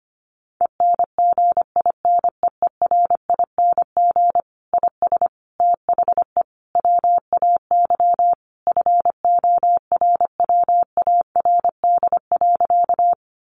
Morse code, heard as a keyed tone, about 25 wpm, 700 Hz. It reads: ENGINEERING IS THE WAY FORWARD.
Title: Sentences from Top 1500 Words 25wpm